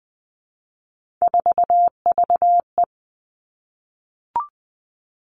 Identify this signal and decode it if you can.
Morse code, heard as a keyed tone, about 20 wpm, 700 Hz. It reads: 4VE
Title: Three Challenging Letter-Number Combinations 20wpm